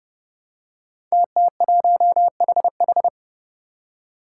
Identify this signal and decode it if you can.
Morse code, heard as a keyed tone, about 30 wpm, 700 Hz. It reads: TT1HH
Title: Five Challenging Letter-Number Combinations 30wpm